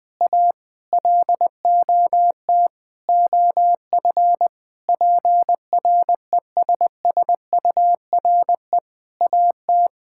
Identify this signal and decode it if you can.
Morse code, heard as a keyed tone, about 20 wpm, 700 Hz. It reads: A LOT OF PRESSURE AT
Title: Sentences from Top 1000 Words - Review Part 6 - 20wpm